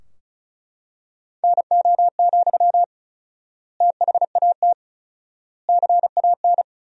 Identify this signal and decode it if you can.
Morse code, heard as a keyed tone, about 36 wpm, 700 Hz. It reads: NO, THAT CAN
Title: 4x Word-Spacing - Sentences from Top 300 Words - 35wpm